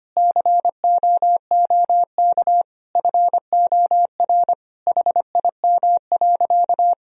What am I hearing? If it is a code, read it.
Morse code, heard as a keyed tone, about 25 wpm, 700 Hz. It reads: COOK FOR HIM.
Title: Sentences from Top 1000 Words - Review Part 4 - 25wpm